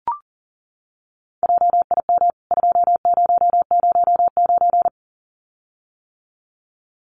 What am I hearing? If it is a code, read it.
Morse code, heard as a keyed tone, about 40 wpm, 700 Hz. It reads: JIM 2009
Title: CWT Exchange 40wpm